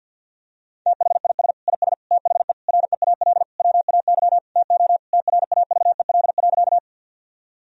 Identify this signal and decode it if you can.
Morse code, heard as a keyed tone, about 50 wpm, 700 Hz. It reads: THIS IS THE REAL WAY TO TRAVEL.
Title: Sentences from Top 1000 Words - Review Part 2 - 50wpm